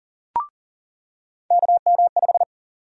Morse code, heard as a keyed tone, about 40 wpm, 700 Hz. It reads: KM5
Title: USA Call Sign Prefixes - 40wpm